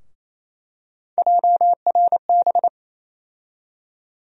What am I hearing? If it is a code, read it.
Morse code, heard as a keyed tone, about 28 wpm, 700 Hz. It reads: JRB
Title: USA Call Sign Postfixes - 3 Letters 28wpm